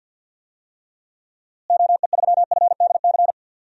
Morse code, heard as a keyed tone, about 50 wpm, 700 Hz. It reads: OE3FDC
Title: DXCC-100 Call Signs - 6 Characters - Repeated 2x - 50wpm